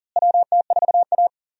Morse code, heard as a keyed tone, about 40 wpm, 700 Hz. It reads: WT4A
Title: USA Call Signs - 4 Characters - Repeated 3x - 40wpm